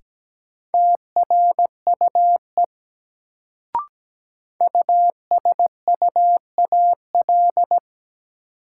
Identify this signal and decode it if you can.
Morse code, heard as a keyed tone, about 17 wpm, 700 Hz. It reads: TRUE USUAL
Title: Sentences from Top 400 Words - New Word Drill - 17wpm